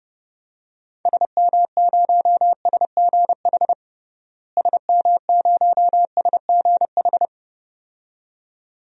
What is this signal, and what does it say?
Morse code, heard as a keyed tone, about 30 wpm, 700 Hz. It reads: SM0SGH SM0SGH
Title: DXCC-100 Call Signs - 6 Characters - Repeated 2x - 30wpm